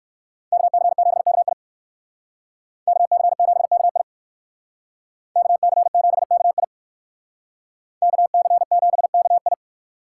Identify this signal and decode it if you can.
Morse code, heard as a keyed tone, about 54 wpm, 700 Hz. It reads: KC7KI KC7KI KC7KI KC7KI
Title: Speed Racing - USA Call Signs 45wpm